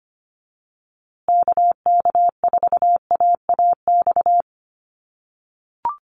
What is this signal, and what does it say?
Morse code, heard as a keyed tone, about 25 wpm, 700 Hz. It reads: KK4AAX
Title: USA Call Signs Repeated 3x 25wpm